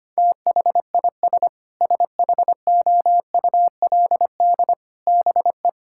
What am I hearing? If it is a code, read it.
Morse code, heard as a keyed tone, about 25 wpm, 700 Hz. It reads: THIS SHOULD BE
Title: Sentences from Top 1000 Words - Review Part 6 - 25wpm